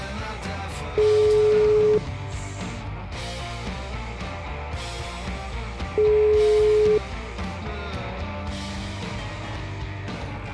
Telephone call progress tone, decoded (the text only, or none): ringback tone